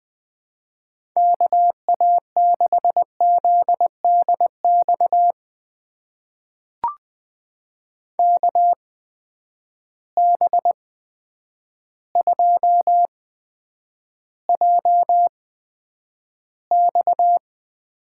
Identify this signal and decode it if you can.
Morse code, heard as a keyed tone, about 21 wpm, spaced as for 9 wpm, 700 Hz. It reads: KA6ZDX KB2JX
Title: USA Call Signs - 8x Character Spacing - 20wpm